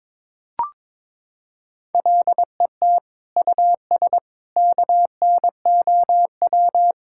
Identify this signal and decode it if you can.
Morse code, heard as a keyed tone, about 22 wpm, 700 Hz. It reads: LET US KNOW
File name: Sentences with 9-Word Phrases - 22wpm